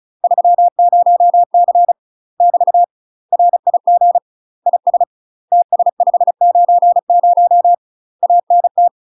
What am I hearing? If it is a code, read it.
Morse code, heard as a keyed tone, about 36 wpm, 700 Hz. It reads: 30C <BT> RIG IS TS590 ANT
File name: UK MASTER QSOs TRAINING / 35WPM / UK MASTER QSOs 0026